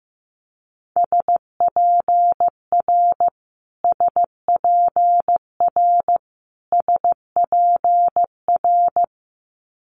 Morse code, heard as a keyed tone, about 16 wpm, 700 Hz. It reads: SPR SPR SPR
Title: Multi-letter Phonemes Repeated 3x 15wpm